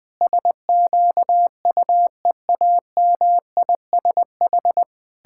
Morse code, heard as a keyed tone, about 20 wpm, 700 Hz. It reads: SQUEAMISH
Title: Pangram Challenge v2 - 20wpm